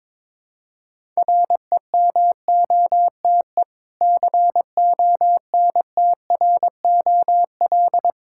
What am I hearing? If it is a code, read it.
Morse code, heard as a keyed tone, about 22 wpm, 690 Hz. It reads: REMOTE CONTROL
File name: Difficult Words with Hints - 22wpm